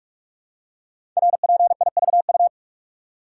Morse code, heard as a keyed tone, about 44 wpm, 700 Hz. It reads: RPIVU
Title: Code Groups - 5 Letters - 45wpm